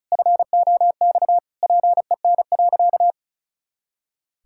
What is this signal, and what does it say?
Morse code, heard as a keyed tone, about 34 wpm, 700 Hz. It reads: FOX PEN.
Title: Pangram Challenge v2 - 35wpm